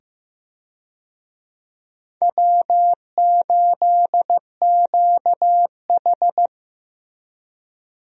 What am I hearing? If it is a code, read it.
Morse code, heard as a keyed tone, about 15 wpm, 700 Hz. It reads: W8QH
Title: USA Call Signs - 4 Characters - 15wpm